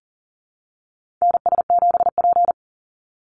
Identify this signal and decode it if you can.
Morse code, heard as a keyed tone, about 40 wpm, 700 Hz. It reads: NS7P
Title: NA SOTA Chaser Callsigns 40wpm